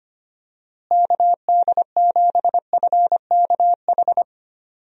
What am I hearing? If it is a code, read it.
Morse code, heard as a keyed tone, about 25 wpm, 700 Hz. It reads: KD7FKH
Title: USA Call Signs - 6 Characters - Repeated 2x - 25wpm